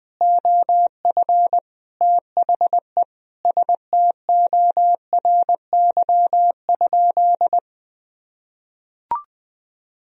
Morse code, heard as a keyed tone, about 20 wpm, 700 Hz. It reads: OF THE STORY?
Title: Sentences from Top 1000 Words - Review Part 1 - 20wpm